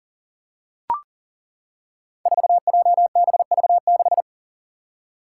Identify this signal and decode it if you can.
Morse code, heard as a keyed tone, about 40 wpm, 700 Hz. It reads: 4JBV6
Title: Five Challenging Letter-Number Combinations 40wpm